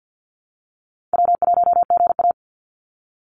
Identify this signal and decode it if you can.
Morse code, heard as a keyed tone, about 50 wpm, 700 Hz. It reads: W1GA
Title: USA Call Signs 50wpm